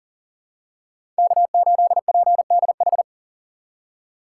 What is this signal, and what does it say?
Morse code, heard as a keyed tone, about 40 wpm, 700 Hz. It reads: K8PDH
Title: USA Call Signs - 5 Characters - 2x Character Spacing - 40wpm